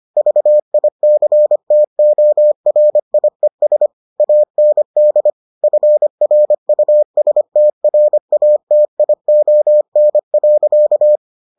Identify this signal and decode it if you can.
Morse code, heard as a keyed tone, about 25 wpm, 600 Hz. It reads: VICTORIES AND FRUSTRATION.